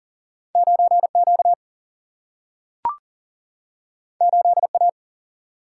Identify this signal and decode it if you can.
Morse code, heard as a keyed tone, about 40 wpm, 700 Hz. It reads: 9Q 8A
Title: Single-Letter Single-Number 40wpm